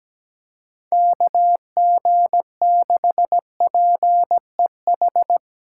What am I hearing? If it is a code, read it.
Morse code, heard as a keyed tone, about 18 wpm, 700 Hz. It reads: KG6PEH
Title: Speed Racing - USA Call Signs 17wpm